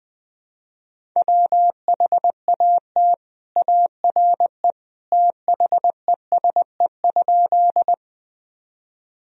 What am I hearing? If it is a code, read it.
Morse code, heard as a keyed tone, about 19 wpm, 700 Hz. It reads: WHAT ARE THESE?
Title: Sentences from Top 1000 Words - Review Part 1 - 20wpm